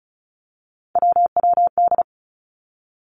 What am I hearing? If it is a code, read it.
Morse code, heard as a keyed tone, about 36 wpm, 700 Hz. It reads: WWD
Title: Three Challenging Letter-Number Combinations 35wpm